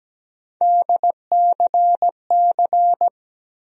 Morse code, heard as a keyed tone, about 17 wpm, 700 Hz. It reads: DCC
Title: USA Call Sign Postfixes - 3 Letters 17wpm